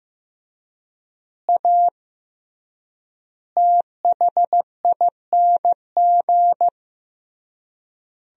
Morse code, heard as a keyed tone, about 14 wpm, 700 Hz. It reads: A THING
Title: 3x Word-Spacing - Sentences from Top 300 Words - 15wpm